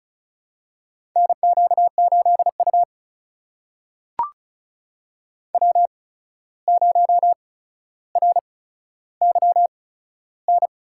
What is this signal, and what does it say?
Morse code, heard as a keyed tone, about 36 wpm, spaced as for 14 wpm, 700 Hz. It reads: NQ8U W0RYN
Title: USA Call Signs - 8x Character Spacing - 35wpm